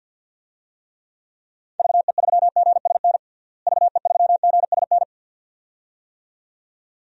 Morse code, heard as a keyed tone, about 52 wpm, 700 Hz. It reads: VE3GSN VE3GSN
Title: DXCC-100 Call Signs - 6 Characters - Repeated 2x - 50wpm